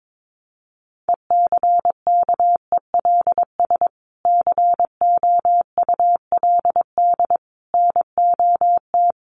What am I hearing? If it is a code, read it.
Morse code, heard as a keyed tone, about 22 wpm, 700 Hz. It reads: ECKELS COULD NOT
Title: A Sound Of Thunder 22wpm